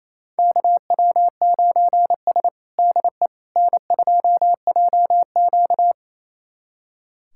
Morse code, heard as a keyed tone, about 28 wpm, 700 Hz. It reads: KW9S DE N2JQ
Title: USA Callsign DE Callsign - 4 Characters - 28wpm